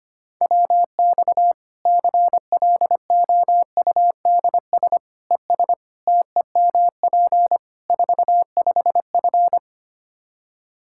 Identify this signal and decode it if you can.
Morse code, heard as a keyed tone, about 25 wpm, 700 Hz. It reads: WX CLOUDS ES TEMP 45F
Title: Practice QSO - Exchange 2 - Info - 25wpm